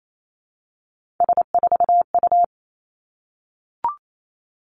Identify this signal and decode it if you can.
Morse code, heard as a keyed tone, about 28 wpm, 700 Hz. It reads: S4U
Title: Three Challenging Letter-Number Combinations 28wpm